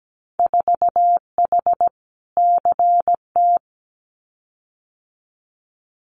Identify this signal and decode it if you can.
Morse code, heard as a keyed tone, about 17 wpm, 700 Hz. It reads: 4H CT
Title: Winter Field Day Exchanges Encore 17wpm